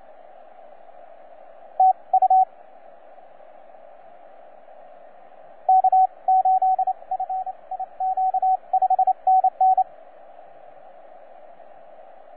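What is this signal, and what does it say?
Morse code, heard as a keyed tone, about 28 wpm, spaced as for 24 wpm, 750 Hz, with radio noise. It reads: TU K8FIQ 5NN
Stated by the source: TZ6BB on 12 metres CW, from DX World.net.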